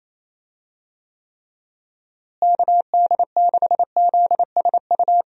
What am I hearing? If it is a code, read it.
Morse code, heard as a keyed tone, about 27 wpm, 700 Hz. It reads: KD6ZSU
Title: USA Call Signs - 6 Characters - Repeated 2x - 28wpm